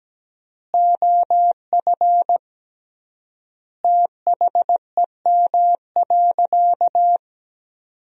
Morse code, heard as a keyed tone, about 17 wpm, 700 Hz. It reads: OF THEM.
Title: 3x Word-Spacing - Sentences from Top 300 Words - 17wpm